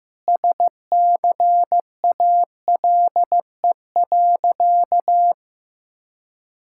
Morse code, heard as a keyed tone, about 15 wpm, 690 Hz. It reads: SCALE.
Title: Sentences from Top 1000 Words - Review Part 5 - 15wpm